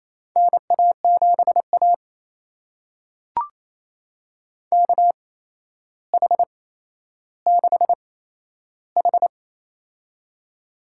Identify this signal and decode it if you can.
Morse code, heard as a keyed tone, about 28 wpm, spaced as for 12 wpm, 700 Hz. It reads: NA7A KH6H
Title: USA Call Signs - 4 Characters - 8x Character Spacing - 28wpm